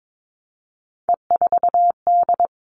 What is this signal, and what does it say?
Morse code, heard as a keyed tone, about 22 wpm, 700 Hz. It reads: E4D
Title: All Cut Numbers - 3-Digit 22wpm